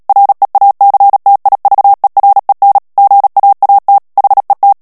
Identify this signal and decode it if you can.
Morse code, heard as a keyed tone, about 37 wpm, 800 Hz. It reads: REACTIVEREN GAAT HET
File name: morse mp3 / HOLLANDAIS texte clair 2 vitesse 35 mots minute